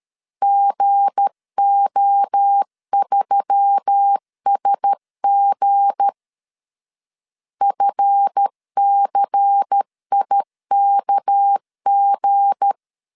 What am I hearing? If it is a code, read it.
Morse code, heard as a keyed tone, about 13 wpm, 800 Hz. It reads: GO3SG FCIKG
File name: LEZIONI / 8 YJ45 / 8 YJ45 12wpm 2151